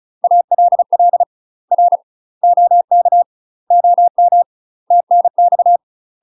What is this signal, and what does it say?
Morse code, heard as a keyed tone, about 36 wpm, 700 Hz. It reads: ALL R OK OM TNX
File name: UK MASTER QSOs TRAINING / 35WPM / UK MASTER QSOs 0041